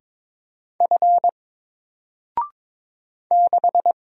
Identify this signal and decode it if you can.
Morse code, heard as a keyed tone, about 22 wpm, 700 Hz. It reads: F 6
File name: Single-Letter-Number-22wpm